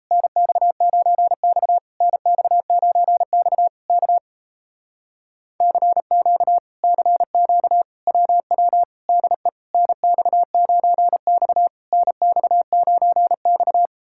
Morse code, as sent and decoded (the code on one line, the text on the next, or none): -. -..- ----. -..- / -. -..- ----. -..- / -.- / -.-. --.- / -.-. --.- / .-- .-- / -.. / -. -..- ----. -..- / -. -..- ----. -..-
NX9X NX9X K CQ CQ WW DE NX9X NX9X